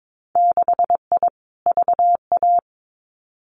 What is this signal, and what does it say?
Morse code, heard as a keyed tone, about 22 wpm, 700 Hz. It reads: 6I VA
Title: Winter Field Day Exchanges Encore 22wpm